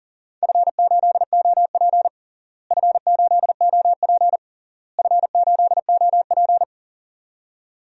Morse code, heard as a keyed tone, about 40 wpm, 700 Hz. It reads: F8OP F8OP F8OP
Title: DXCC-100 Call Signs - 4 Characters - Repeated 3x - 40wpm